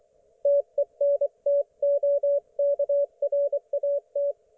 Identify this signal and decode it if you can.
Morse code, heard as a keyed tone, about 24 wpm, 570 Hz. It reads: TENTOKRAT
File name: Archiv / CUC224 / ok1fhi cul224